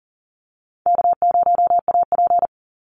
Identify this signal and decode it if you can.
Morse code, heard as a keyed tone, about 40 wpm, 700 Hz. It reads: K0AP
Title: Call Signs Repeated 2x - 40wpm